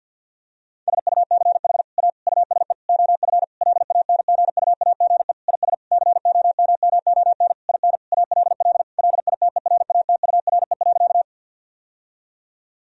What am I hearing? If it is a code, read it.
Morse code, heard as a keyed tone, about 50 wpm, 700 Hz. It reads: SUCH A USE OF LANGUAGE IS COMMON IN ALL LITERATURE.
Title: Sentences from Top 1000 Words - Review Part 3 - 50wpm